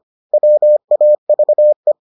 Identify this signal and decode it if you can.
Morse code, heard as a keyed tone, about 25 wpm, 600 Hz. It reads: WAVE